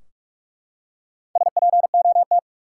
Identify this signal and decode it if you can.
Morse code, heard as a keyed tone, about 45 wpm, 700 Hz. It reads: SPOT
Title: Four Letter Words - 45wpm